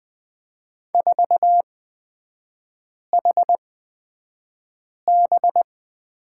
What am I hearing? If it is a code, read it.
Morse code, heard as a keyed tone, about 20 wpm, spaced as for 5 wpm, 700 Hz. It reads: 4HB
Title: 236 Lesson 38e - Letter q - Three Characters 20wpm